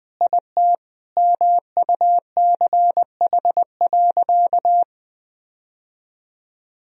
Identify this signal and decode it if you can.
Morse code, heard as a keyed tone, about 20 wpm, 700 Hz. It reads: IT MUCH.
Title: Sentences from Top 1000 Words - Review Part 1 - 20wpm